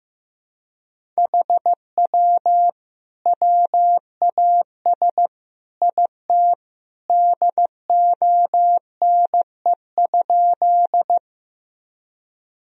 Morse code, heard as a keyed tone, about 15 wpm, 700 Hz. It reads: HW WAS IT DONE?
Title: Sentences from Top 500 Words and Abbreviations 15wpm